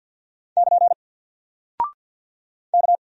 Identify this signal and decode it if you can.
Morse code, heard as a keyed tone, about 50 wpm, 700 Hz. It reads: <KN> K
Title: Single Character (Letter:Number:Punctuation:Prosign) 50wpm